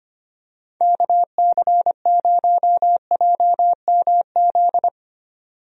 Morse code, heard as a keyed tone, about 25 wpm, 700 Hz. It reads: KC0JMZ
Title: USA Call Signs - 6 Characters - 25wpm